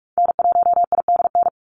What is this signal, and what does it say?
Morse code, heard as a keyed tone, about 44 wpm, 700 Hz. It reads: N1IDN
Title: NA SOTA Activator Callsigns 45wpm